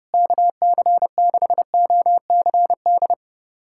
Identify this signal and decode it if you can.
Morse code, heard as a keyed tone, about 32 wpm, 700 Hz. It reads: KC6OCD
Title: USA Callsign DE Callsign - 6 Characters - 30wpm